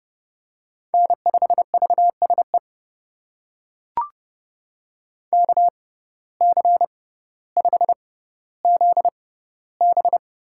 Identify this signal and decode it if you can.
Morse code, heard as a keyed tone, about 31 wpm, spaced as for 16 wpm, 700 Hz. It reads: N5VSE KC5ZB
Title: USA Call Signs - 5 Characters - 6x Character Spacing - 30wpm